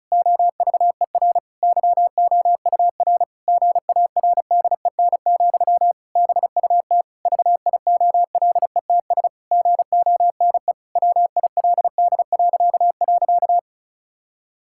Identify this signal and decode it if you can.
Morse code, heard as a keyed tone, about 36 wpm, 700 Hz. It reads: OVER YOUR GARDEN, BUT VIOLETS GONE WILD..
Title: Pangram Challenge v2 - 35wpm